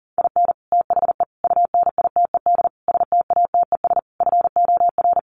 Morse code, heard as a keyed tone, about 40 wpm, 700 Hz. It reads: IN THE UNITED STATES FOR
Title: Sets of 5 Words Encore2 40wpm